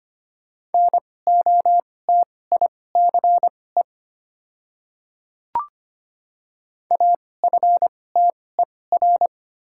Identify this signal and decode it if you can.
Morse code, heard as a keyed tone, about 25 wpm, spaced as for 18 wpm, 700 Hz. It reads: NOTICE AFTER
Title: 2x Character Spacing - Top 500 Words - 25wpm